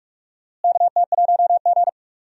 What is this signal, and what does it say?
Morse code, heard as a keyed tone, about 42 wpm, 700 Hz. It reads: KT1G
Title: USA Call Signs - 4 Characters - Repeated 2x - 45wpm